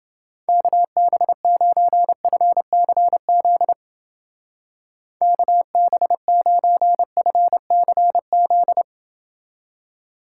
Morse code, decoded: KB9FCZ KB9FCZ

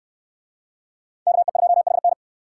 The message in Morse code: KE2HN

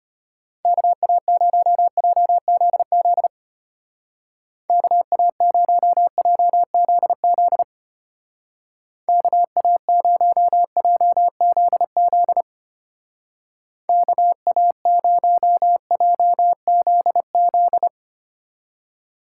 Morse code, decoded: KA0JZZ KA0JZZ KA0JZZ KA0JZZ

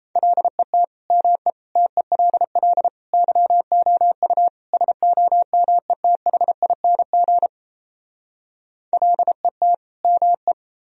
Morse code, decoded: LET ME TELL YOU SOMETHING LET ME

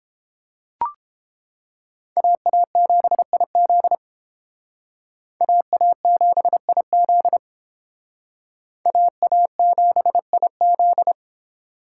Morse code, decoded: AA7IZ AA7IZ AA7IZ

